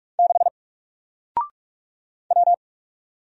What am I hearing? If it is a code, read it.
Morse code, heard as a keyed tone, about 44 wpm, 700 Hz. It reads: 6 W
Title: Single Letter-Number 45wpm